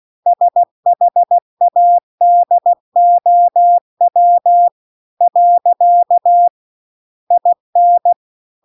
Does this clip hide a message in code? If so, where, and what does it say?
Yes: Morse code from 0.3 to 8.1 s, heard as a keyed tone, about 16 wpm, 700 Hz. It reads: SHADOW . IN